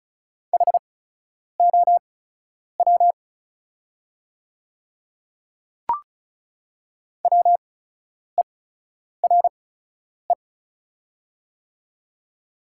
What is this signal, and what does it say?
Morse code, heard as a keyed tone, about 36 wpm, spaced as for 10 wpm, 700 Hz. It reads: HOW WERE
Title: 8x Character Spacing - Top 100 Words - 35wpm